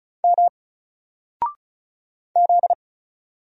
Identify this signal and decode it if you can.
Morse code, heard as a keyed tone, about 33 wpm, 700 Hz. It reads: M Z
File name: Single-Letter-Number-35wpm